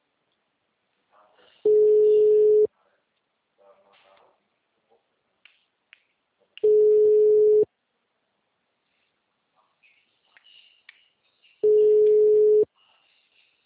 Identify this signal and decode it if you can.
Ringback tone (Europe, 425 Hz), with speech.
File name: out- -2035- - - .30879